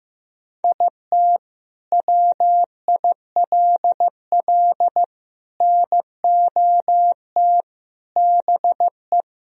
Morse code, heard as a keyed tone, about 15 wpm, 700 Hz. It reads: IT WILL NOT BE
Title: Sentences from Top 1000 Words - Review Part 1 - 15wpm